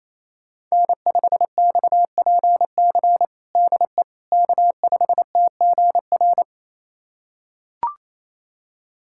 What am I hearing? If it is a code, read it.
Morse code, heard as a keyed tone, about 28 wpm, 700 Hz. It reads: N5XPC DE K5TGR
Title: USA Callsign DE Callsign - 5 Characters - 28wpm